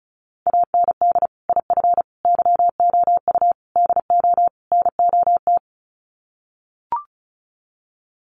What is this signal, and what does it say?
Morse code, heard as a keyed tone, about 34 wpm, 700 Hz. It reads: AND IF YOU DO NOT
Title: Sets of 5 Words 35wpm